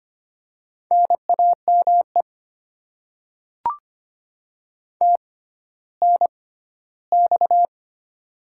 Morse code, heard as a keyed tone, about 25 wpm, spaced as for 14 wpm, 700 Hz. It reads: NAME TNX
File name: Top 20 QSO Elements - 6x Character Spacing - 25wpm